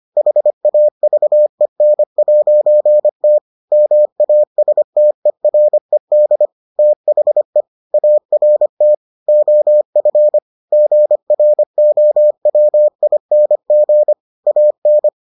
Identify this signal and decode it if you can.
Morse code, heard as a keyed tone, about 25 wpm, 600 Hz. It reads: HAVEN'T MASTERED THE ART OF GROWING AN